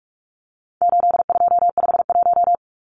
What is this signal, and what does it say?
Morse code, heard as a keyed tone, about 44 wpm, 700 Hz. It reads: 8251
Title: Numbers - 4-Digit 45wpm